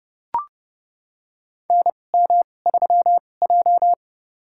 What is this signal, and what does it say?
Morse code, heard as a keyed tone, about 31 wpm, spaced as for 22 wpm, 700 Hz. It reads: NM3J
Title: USA Call Signs - 2x Character Spacing - 30wpm